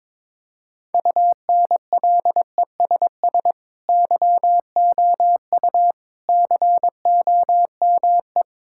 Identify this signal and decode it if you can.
Morse code, heard as a keyed tone, about 22 wpm, 700 Hz. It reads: UNLESS YOU COME
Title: Sentences from Top 1000 Words - Review Part 6 - 22wpm